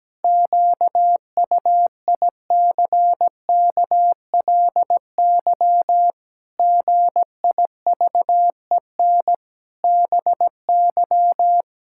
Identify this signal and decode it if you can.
Morse code, heard as a keyed tone, about 17 wpm, 700 Hz. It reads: QUICKLY GIVEN BY
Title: Pangram Challenge v2 - 17wpm